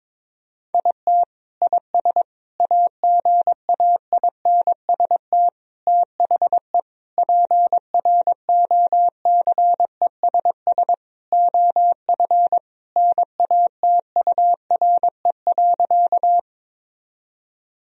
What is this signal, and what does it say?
Morse code, heard as a keyed tone, about 22 wpm, 700 Hz. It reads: IT IS AGAINST THE PROCESS OF NATURE.
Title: Sentences from Top 1000 Words - Review Part 3 - 22wpm